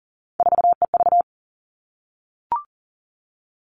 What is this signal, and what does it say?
Morse code, heard as a keyed tone, about 40 wpm, 700 Hz. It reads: 4EV
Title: All Cut Numbers - 3-Digit 40wpm